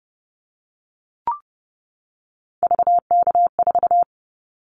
Morse code, heard as a keyed tone, about 30 wpm, 700 Hz. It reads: VK4
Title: Callsign Prefixes for Top 50 DXCC - 30wpm